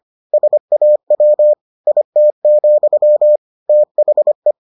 Morse code, heard as a keyed tone, about 25 wpm, 600 Hz. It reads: SAW IT, THE